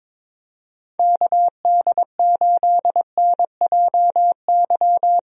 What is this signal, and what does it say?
Morse code, heard as a keyed tone, about 22 wpm, 700 Hz. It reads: KD8NJY
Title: USA Call Signs - 6 Characters - Repeated 2x - 22wpm